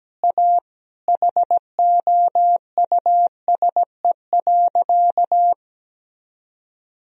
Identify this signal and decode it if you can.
Morse code, heard as a keyed tone, about 17 wpm, 700 Hz. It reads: A HOUSE.
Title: Sentences from Top 1000 Words - Review Part 4 - 17wpm